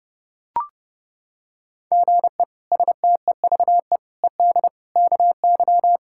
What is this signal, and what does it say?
Morse code, heard as a keyed tone, about 32 wpm, 700 Hz. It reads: GE STEVE ED KY
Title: K1USN SST QSO Practice - 30wpm